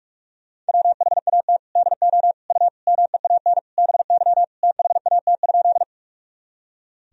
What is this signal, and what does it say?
Morse code, heard as a keyed tone, about 46 wpm, 700 Hz. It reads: WHAT DO U MEAN BY THAT?
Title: Sentences from Top 200 Words and Abbreviations 45wpm